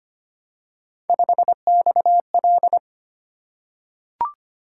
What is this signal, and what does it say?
Morse code, heard as a keyed tone, about 25 wpm, 700 Hz. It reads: 5XL
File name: Three Letter-Number Combinations - 25wpm